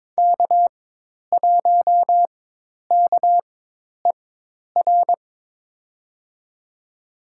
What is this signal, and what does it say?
Morse code, heard as a keyed tone, about 21 wpm, spaced as for 10 wpm, 700 Hz. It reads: K1KER
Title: USA Call Signs - 5 Characters - 4x Character Spacing - 22wpm